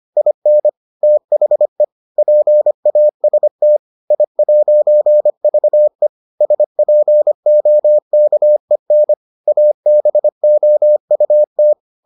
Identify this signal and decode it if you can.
Morse code, heard as a keyed tone, about 25 wpm, 600 Hz. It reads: IN THE PAST I'VE SPOKEN ABOUT